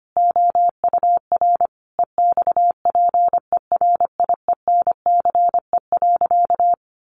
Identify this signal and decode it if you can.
Morse code, heard as a keyed tone, about 25 wpm, 700 Hz. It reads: OUR EXPERIENCE.
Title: Sentences from Top 1200 Words 25wpm